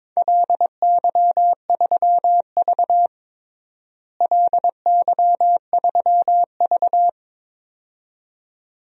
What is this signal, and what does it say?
Morse code, heard as a keyed tone, about 21 wpm, 700 Hz. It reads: LY3V LY3V
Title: DXCC-100 Call Signs - 4 Characters - Repeated 2x - 22wpm